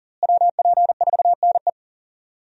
Morse code, heard as a keyed tone, about 40 wpm, 700 Hz. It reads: WP4NE
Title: USA Call Signs - 5 Characters - Repeated 2x - 40wpm